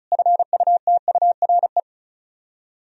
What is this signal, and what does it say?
Morse code, heard as a keyed tone, about 40 wpm, 700 Hz. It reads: FUTURE
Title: Speed-Racing - Sets of 3 Words 35wpm